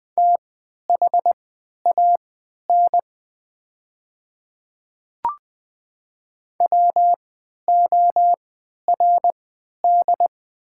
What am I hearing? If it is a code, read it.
Morse code, heard as a keyed tone, about 20 wpm, spaced as for 11 wpm, 690 Hz. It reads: THAN WORD
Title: 3x Character Spacing - Top 200 Words - 20wpm